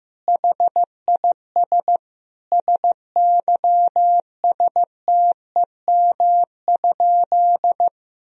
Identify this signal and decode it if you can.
Morse code, heard as a keyed tone, about 15 wpm, 700 Hz. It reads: HIS SYSTEM?
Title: Sentences from Top 1000 Words - Review Part 3 - 15wpm